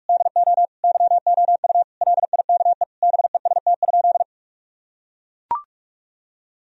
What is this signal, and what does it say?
Morse code, heard as a keyed tone, about 42 wpm, 700 Hz. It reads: DO YOU LIKE BEST?
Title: Sentences from Top 1000 Words - Review Part 1 - 45wpm